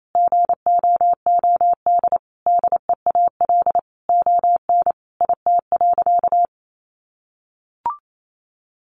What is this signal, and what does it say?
Morse code, heard as a keyed tone, about 28 wpm, 700 Hz. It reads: GOOD DEAL ON IT.
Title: Sentences from Top 1300 Words 28wpm